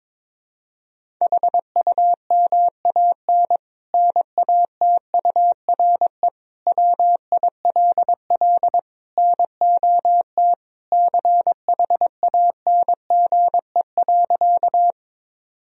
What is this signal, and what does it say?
Morse code, heard as a keyed tone, about 22 wpm, 700 Hz. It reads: HUMAN NATURE WILL NOT CHANGE.
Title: Sentences from Top 1000 Words - Review Part 3 - 22wpm